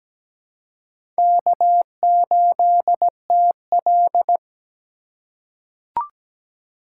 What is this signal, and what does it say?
Morse code, heard as a keyed tone, about 17 wpm, 700 Hz. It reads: K8TL
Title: USA Call Signs - 4 Characters - 3x Character Spacing - 17wpm